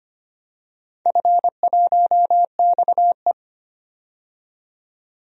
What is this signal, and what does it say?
Morse code, heard as a keyed tone, about 25 wpm, 700 Hz. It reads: F1XE
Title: DXCC-100 Call Signs - 4 Characters - 25wpm